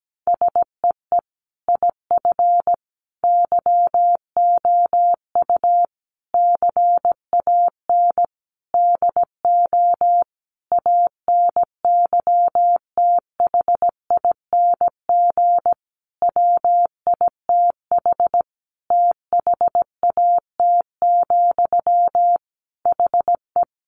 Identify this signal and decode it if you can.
Morse code, heard as a keyed tone, about 17 wpm, 700 Hz. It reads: SEE IF YOU CAN DO ANYTHING WITH THAT, HE
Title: Sentences from Top 1000 Words 17wpm